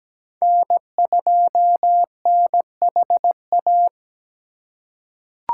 Morse code, heard as a keyed tone, about 17 wpm, 700 Hz. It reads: N2NHA
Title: USA Call Signs - 5 Characters - 3x Character Spacing - 17wpm